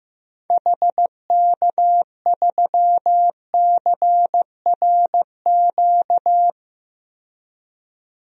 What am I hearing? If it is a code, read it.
Morse code, heard as a keyed tone, about 15 wpm, 700 Hz. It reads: HK3CRQ
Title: DXCC-100 Call Signs - 6 Characters - 15wpm